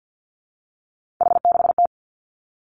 Morse code, heard as a keyed tone, about 50 wpm, 700 Hz. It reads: H6T
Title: Three Challenging Letter-Number Combinations 50wpm